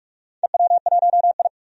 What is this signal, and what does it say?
Morse code, heard as a keyed tone, about 41 wpm, 700 Hz. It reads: EW1I
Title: DXCC-100 Call Signs - 4 Characters - 45wpm